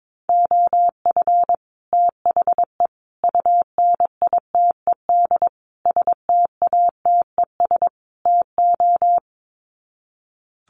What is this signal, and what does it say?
Morse code, heard as a keyed tone, about 22 wpm, 700 Hz. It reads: OF THE UNITED STATES TO
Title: Sets of 5 Words Encore 22wpm